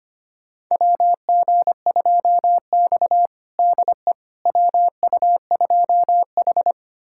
Morse code, heard as a keyed tone, about 25 wpm, 700 Hz. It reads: WG2X DE WU2H
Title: USA Callsign DE Callsign - 4 Characters - 25wpm